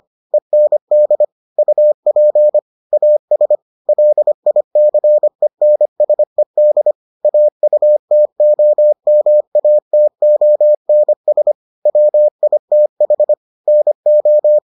Morse code, heard as a keyed tone, about 25 wpm, 600 Hz. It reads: END UP AS LICENSED AUTOMATONS WITH NO